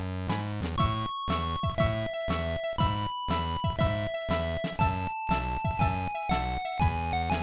There is no code.